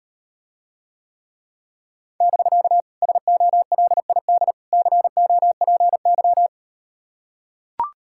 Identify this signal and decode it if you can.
Morse code, heard as a keyed tone, about 38 wpm, 690 Hz. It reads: <BK> SOLID COPY